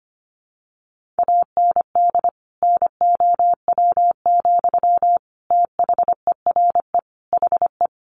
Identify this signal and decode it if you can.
Morse code, heard as a keyed tone, about 25 wpm, 700 Hz. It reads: AND NOW, THERE HE
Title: Sentences from Top 1500 Words 25wpm